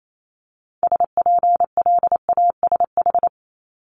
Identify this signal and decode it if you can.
Morse code, heard as a keyed tone, about 28 wpm, 700 Hz. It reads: SPLASH
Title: Multi-letter Phonemes 28wpm